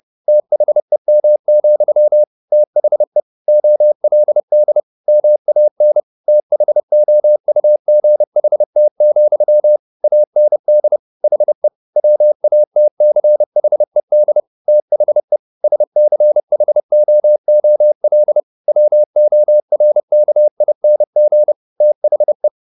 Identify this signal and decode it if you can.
Morse code, heard as a keyed tone, about 30 wpm, 600 Hz. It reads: THEM, THE OLD MAN THOUGHT, AND HE WATCHED THE SCHOOL WORKING THE
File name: the old man and the sea / the old man and the sea 1 30wpm